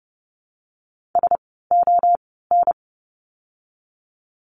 Morse code, heard as a keyed tone, about 30 wpm, spaced as for 17 wpm, 700 Hz. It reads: SON
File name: Lesson 17f - Letter u - Words 30wpm